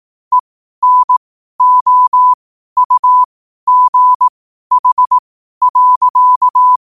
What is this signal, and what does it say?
Morse code, heard as a keyed tone, about 18 wpm, spaced as for 12 wpm, 1000 Hz. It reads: ENOUGH.
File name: qotdfiles / 12wpm